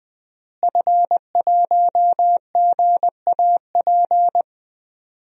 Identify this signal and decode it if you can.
Morse code, heard as a keyed tone, about 21 wpm, 700 Hz. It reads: F1GAP
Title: DXCC-100 Call Signs - 5 Characters - Repeated 3x - 20wpm